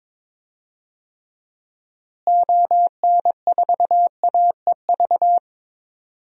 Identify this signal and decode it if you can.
Morse code, heard as a keyed tone, about 22 wpm, 700 Hz. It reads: ON4AEV
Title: DXCC-100 Call Signs - 6 Characters - Repeated 3x - 22wpm